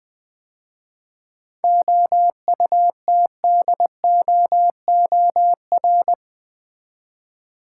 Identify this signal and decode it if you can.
Morse code, heard as a keyed tone, about 20 wpm, 690 Hz. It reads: OUTDOOR
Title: 7 Letter Words - First 3 Letters Given - 20wpm